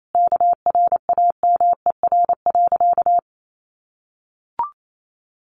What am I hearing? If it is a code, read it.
Morse code, heard as a keyed tone, about 28 wpm, 700 Hz. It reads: KRAMER.
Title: A Sound Of Thunder 28wpm